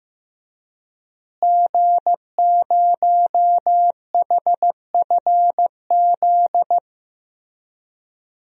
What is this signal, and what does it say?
Morse code, heard as a keyed tone, about 16 wpm, 700 Hz. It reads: G0HFZ
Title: DXCC-100 Call Signs - 5 Characters - Repeated 3x - 15wpm